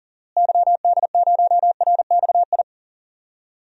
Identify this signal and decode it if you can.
Morse code, heard as a keyed tone, about 40 wpm, 700 Hz. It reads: YD0RXI
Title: DXCC-100 Call Signs - 6 Characters - Repeated 2x - 40wpm